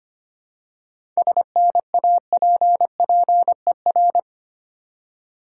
Morse code, heard as a keyed tone, about 25 wpm, 700 Hz. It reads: SNAPPER
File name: practice / 25wpm / Difficult Words with Hints - 25wpm